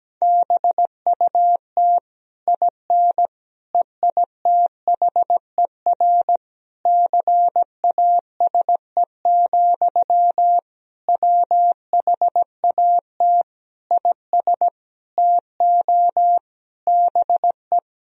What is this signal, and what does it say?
Morse code, heard as a keyed tone, about 17 wpm, 700 Hz. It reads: BUT IN EITHER CASE, WHAT IS TO BE
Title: Sentences from Top 1000 Words - Review Part 4 - 17wpm